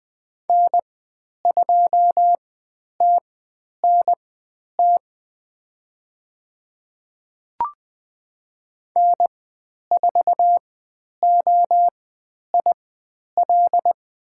Morse code, heard as a keyed tone, about 20 wpm, spaced as for 10 wpm, 700 Hz. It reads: N2TNT N4OIL
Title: 095 Lesson 16g - Letter c - Call Signs 20wpm